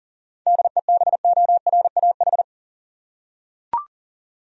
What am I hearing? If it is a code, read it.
Morse code, heard as a keyed tone, about 39 wpm, 700 Hz. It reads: DEBORAH
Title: USA Common Names - 40wpm